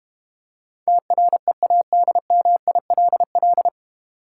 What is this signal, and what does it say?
Morse code, heard as a keyed tone, about 32 wpm, 700 Hz. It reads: TREADMILL